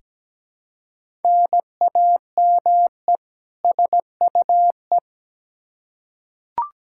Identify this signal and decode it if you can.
Morse code, heard as a keyed tone, about 17 wpm, 700 Hz. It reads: NAME SUE
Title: Practice QSO - Exchange 1 - Beginning - 17wpm